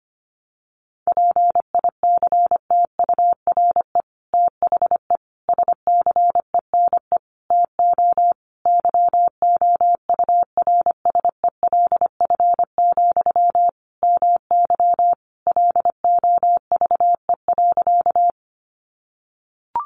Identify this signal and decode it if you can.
Morse code, heard as a keyed tone, about 25 wpm, 700 Hz. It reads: PICTURE THE SCENE TO YOURSELF, MY LOVE.
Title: Sentences from Top 1100 Words 25wpm